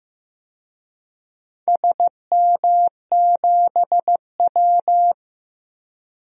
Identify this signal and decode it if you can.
Morse code, heard as a keyed tone, about 15 wpm, 700 Hz. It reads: SM7W
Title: DXCC-100 Call Signs - 4 Characters - 15wpm